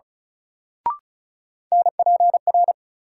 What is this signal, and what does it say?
Morse code, heard as a keyed tone, about 36 wpm, 700 Hz. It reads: NPR
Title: Sets of 5 Words - New Word Drill - 35wpm